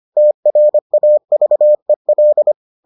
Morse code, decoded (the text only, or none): TRAVEL